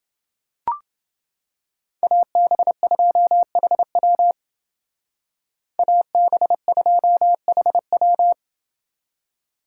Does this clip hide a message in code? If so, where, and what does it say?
Yes: Morse code from 2.0 to 8.3 s, heard as a keyed tone, about 28 wpm, 700 Hz. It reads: AB2HW AB2HW